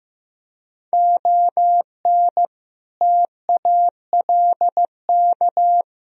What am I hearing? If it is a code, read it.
Morse code, heard as a keyed tone, about 15 wpm, 700 Hz. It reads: ON TALK
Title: Speed-Racing - Sets of 5 Words 15wpm